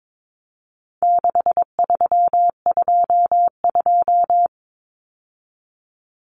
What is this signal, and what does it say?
Morse code, heard as a keyed tone, about 22 wpm, 700 Hz. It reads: 6322
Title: Numbers - 4-Digit 22wpm